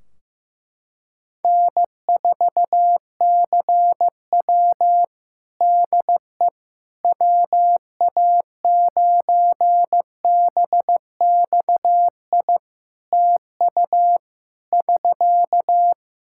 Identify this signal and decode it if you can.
Morse code, heard as a keyed tone, about 15 wpm, 700 Hz. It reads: N4CW DE WA9BXI TU <SK>
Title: Practice QSO - Exchange 4 - Ending - 15wpm